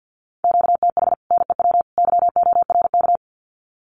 Morse code, heard as a keyed tone, about 50 wpm, 700 Hz. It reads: QTH NEW YORK
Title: Common QSO Phrases 50wpm